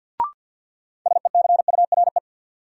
Morse code, heard as a keyed tone, about 49 wpm, 700 Hz. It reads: SECURE
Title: Sentences from Top 1200 Words - New Word Drill Alt - 50wpm